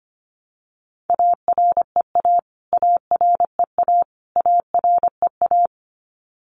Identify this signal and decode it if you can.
Morse code, heard as a keyed tone, about 25 wpm, 700 Hz. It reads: AREA AREA AREA
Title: Top 500 Words Repeated 25wpm